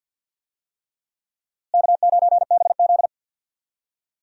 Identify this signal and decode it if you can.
Morse code, heard as a keyed tone, about 50 wpm, 700 Hz. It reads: K9BZ
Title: USA Call Signs - 4 Characters - Repeated 3x - 50wpm